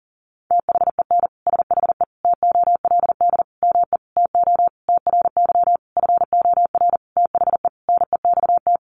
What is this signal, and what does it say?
Morse code, heard as a keyed tone, about 40 wpm, 700 Hz. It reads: THEN SHE TOLD ME TO TRY FOR THE NEXT
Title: Sentences from Top 400 Words 40wpm